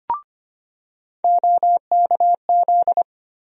Morse code, heard as a keyed tone, about 25 wpm, 700 Hz. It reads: OKZ
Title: USA Call Sign Postfixes - 3 Letters 25wpm